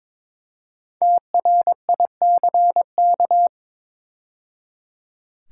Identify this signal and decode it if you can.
Morse code, heard as a keyed tone, about 23 wpm, 700 Hz. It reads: TRICK
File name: 5 Letter Words - First 3 Letters Given - 22wpm